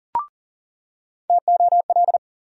Morse code, heard as a keyed tone, about 40 wpm, 700 Hz. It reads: TOL
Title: USA Call Sign Postfixes - 3 Letters 40wpm